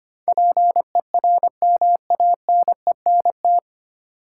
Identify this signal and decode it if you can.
Morse code, heard as a keyed tone, about 25 wpm, 700 Hz. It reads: PERMANENT
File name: Nine Letter Words - 25wpm